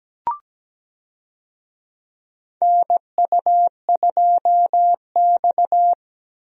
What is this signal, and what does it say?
Morse code, heard as a keyed tone, about 17 wpm, 700 Hz. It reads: NU2X
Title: USA Call Signs - 4 Characters - Repeated 3x - 17wpm